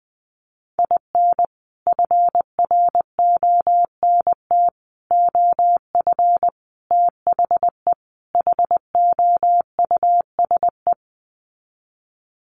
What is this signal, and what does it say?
Morse code, heard as a keyed tone, about 20 wpm, 700 Hz. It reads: IN FRONT OF THE HOUSE
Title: Sets of 5 Words Encore2 20wpm